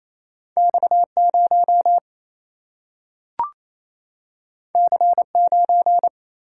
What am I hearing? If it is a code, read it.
Morse code, heard as a keyed tone, about 28 wpm, 700 Hz. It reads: X0 C9
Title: Single-Letter Single-Number 28wpm